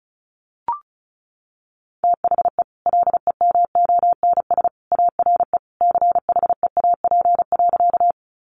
Morse code, heard as a keyed tone, about 34 wpm, 700 Hz. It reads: THE LEMONS ARE CHEAP.
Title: Additional Easy Phrases 35wpm